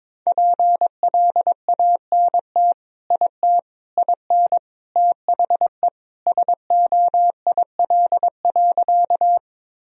Morse code, heard as a keyed tone, about 22 wpm, 700 Hz. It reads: PLANT IT IN THE SOIL.
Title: Sentences from Top 1000 Words - Review Part 4 - 22wpm